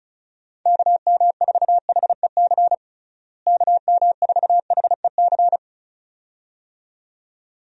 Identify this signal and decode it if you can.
Morse code, heard as a keyed tone, about 34 wpm, 700 Hz. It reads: KM4HEC KM4HEC
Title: USA Call Signs - 6 Characters - Repeated 2x - 35wpm